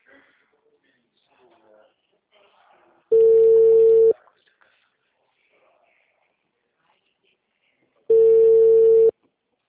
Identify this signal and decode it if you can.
Ringback tone (China, 450 Hz), with speech.
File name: out- -2020- - - .37521